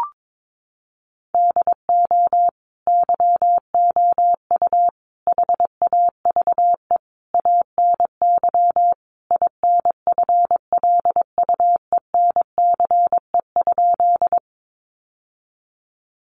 Morse code, heard as a keyed tone, about 22 wpm, 700 Hz. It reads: DO YOU HAVE ANY INFLUENCE?
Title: Sentences from Top 1100 Words 22wpm